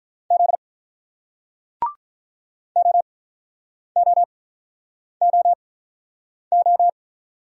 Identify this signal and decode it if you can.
Morse code, heard as a keyed tone, about 48 wpm, spaced as for 9 wpm, 700 Hz. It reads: Z OOOO